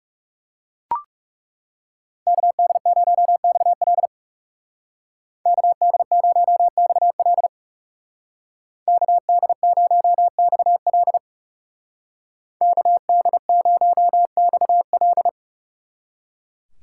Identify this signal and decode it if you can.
Morse code, heard as a keyed tone, about 35 wpm, 700 Hz. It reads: KD0XL KD0XL KD0XL KD0XL
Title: Speed Racing - USA 2x2 Call Signs 30wpm